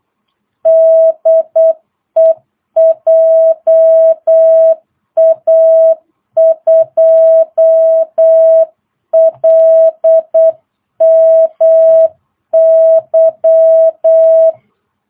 Morse code, heard as a keyed tone, about 8 wpm, 660 Hz. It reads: DE JA2LMY
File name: jh8cbh / 2022cw